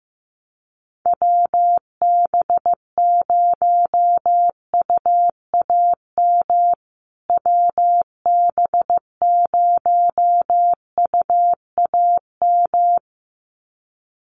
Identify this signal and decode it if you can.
Morse code, heard as a keyed tone, about 15 wpm, 700 Hz. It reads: WB0UAM WB0UAM
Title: USA Call Signs Repeated 2x 15wpm